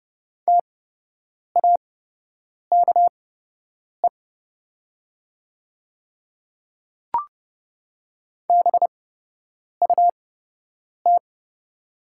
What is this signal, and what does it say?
Morse code, heard as a keyed tone, about 30 wpm, spaced as for 8 wpm, 700 Hz. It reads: TAKE BUT
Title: 8x Character Spacing - Top 200 Words - 30wpm